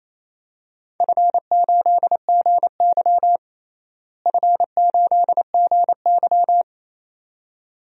Morse code, heard as a keyed tone, about 28 wpm, 700 Hz. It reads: F8GY F8GY
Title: DXCC-100 Call Signs - 4 Characters - Repeated 2x - 28wpm